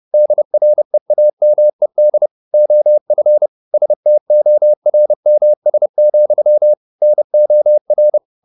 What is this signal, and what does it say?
Morse code, heard as a keyed tone, about 30 wpm, 600 Hz. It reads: DREAMED OF STORMS, NOR
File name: the old man and the sea / the old man and the sea 1 30wpm